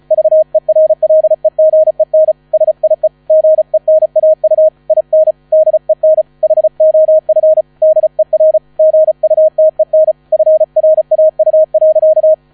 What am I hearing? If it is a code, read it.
Morse code, heard as a keyed tone, about 34 wpm, 630 Hz. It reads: VERLEGEN SIE GENAU IN DEN HOF DER GUTEN FRAU.
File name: 35wpm